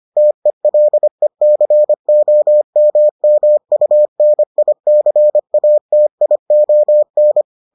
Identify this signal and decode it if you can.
Morse code, heard as a keyed tone, about 25 wpm, 600 Hz. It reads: TELECOMMUNICATION